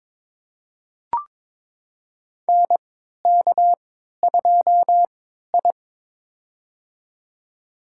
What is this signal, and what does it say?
Morse code, heard as a keyed tone, about 22 wpm, spaced as for 13 wpm, 700 Hz. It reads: NK2I
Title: USA Call Signs - 4 Characters - 3x Character Spacing - 22wpm